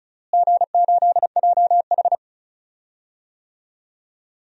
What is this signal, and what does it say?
Morse code, heard as a keyed tone, about 34 wpm, 700 Hz. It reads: G8JH
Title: DXCC-100 Call Signs - 4 Characters - Repeated 3x - 35wpm